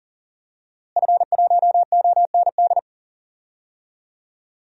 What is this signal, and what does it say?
Morse code, heard as a keyed tone, about 40 wpm, 700 Hz. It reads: F1OND
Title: DXCC-100 Call Signs - 5 Characters - Repeated 2x - 40wpm